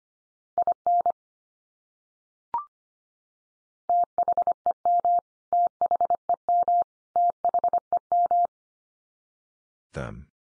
Morse code, heard as a keyed tone, about 25 wpm, 700 Hz. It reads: IN THEM THEM THEM
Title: Top 100 Words 25wpm x3